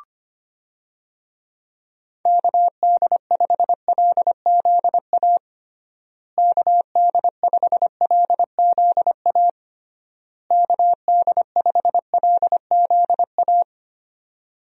Morse code, heard as a keyed tone, about 25 wpm, 700 Hz. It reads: KD5LZA KD5LZA KD5LZA
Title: USA Call Signs - 6 Characters - Repeated 3x - 25wpm